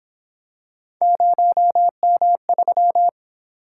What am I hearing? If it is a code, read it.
Morse code, heard as a keyed tone, about 28 wpm, 700 Hz. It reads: 0M3
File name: Speed-Racing - Three Letter-Number Combinations - 22wpm